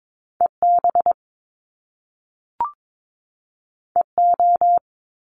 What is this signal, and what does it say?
Morse code, heard as a keyed tone, about 22 wpm, 700 Hz. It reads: EB EO
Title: Top Two Letter Combinations in Words 22wpm